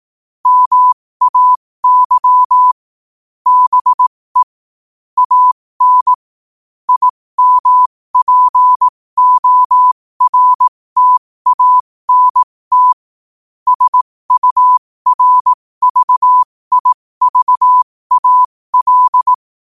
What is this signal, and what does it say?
Morse code, heard as a keyed tone, about 18 wpm, spaced as for 15 wpm, 1000 Hz. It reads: MAY BE AN IMPORTANT SURVIVAL
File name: qotdfiles / 15wpm